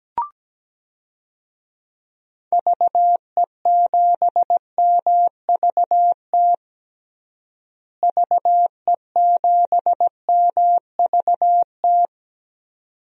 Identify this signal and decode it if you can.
Morse code, heard as a keyed tone, about 17 wpm, 700 Hz. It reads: VE7MVT VE7MVT
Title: DXCC-100 Call Signs - 6 Characters - Repeated 2x - 17wpm